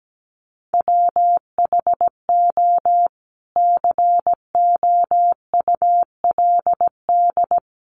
Morse code, heard as a keyed tone, about 17 wpm, 700 Hz. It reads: WHO COULD
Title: Sentences from Top 1300 Words 17wpm